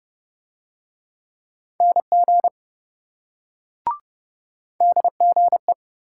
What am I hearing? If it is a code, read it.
Morse code, heard as a keyed tone, about 26 wpm, 700 Hz. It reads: NG DGE